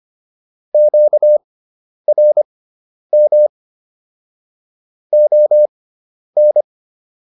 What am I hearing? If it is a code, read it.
Morse code, heard as a keyed tone, about 25 wpm, spaced as for 10 wpm, 600 Hz. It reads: QRM ON